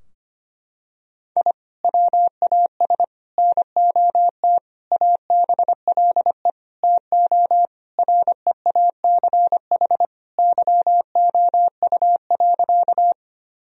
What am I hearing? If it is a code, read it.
Morse code, heard as a keyed tone, about 25 wpm, 700 Hz. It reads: I WAS NOT ABLE TO REACH YOU.
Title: Sentences from Top 1000 Words - Review Part 2 - 25wpm